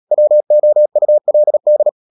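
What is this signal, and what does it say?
Morse code, heard as a keyed tone, about 37 wpm, 600 Hz. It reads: WOULD
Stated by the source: Morse code practice at 35–39 wpm sending English text of Ken Follett's novel Never, from CW med Gnister.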